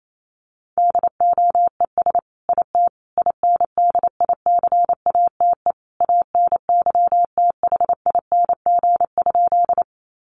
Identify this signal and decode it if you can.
Morse code, heard as a keyed tone, about 28 wpm, 700 Hz. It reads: DOES IT INDICATE ANYTHING?
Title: Sentences from Top 1300 Words 28wpm